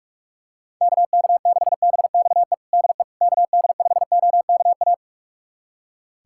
Morse code, heard as a keyed tone, about 46 wpm, 700 Hz. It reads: KK6BXE DE KD5OKA
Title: USA Callsign DE Callsign - 6 Characters - 45wpm